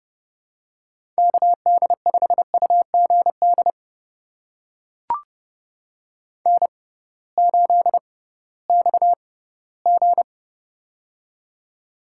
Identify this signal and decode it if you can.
Morse code, heard as a keyed tone, about 32 wpm, spaced as for 18 wpm, 700 Hz. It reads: KD5UGD N8XG
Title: USA Call Signs - 6x Character Spacing - 30wpm